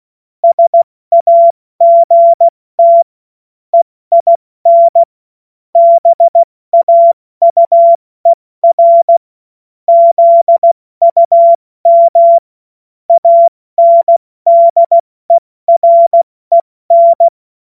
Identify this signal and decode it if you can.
Morse code, heard as a keyed tone, about 16 wpm, 680 Hz. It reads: SAGT EIN BAUER ZUM ANDEREN
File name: Witz-22-Juli-2025-70-F80